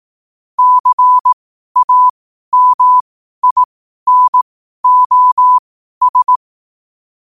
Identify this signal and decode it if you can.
Morse code, heard as a keyed tone, about 18 wpm, spaced as for 12 wpm, 1000 Hz. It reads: CAMINOS
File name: qotdfiles / 12wpm